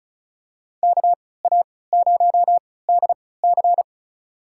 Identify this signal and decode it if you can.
Morse code, heard as a keyed tone, about 34 wpm, spaced as for 20 wpm, 700 Hz. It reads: KA0DC
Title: USA Call Signs - 5 Characters - 3x Character Spacing - 35wpm